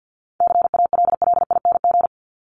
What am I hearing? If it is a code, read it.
Morse code, heard as a keyed tone, about 49 wpm, 700 Hz. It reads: CALLING
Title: ING Suffix - Top 100 Words with Base Words 50wpm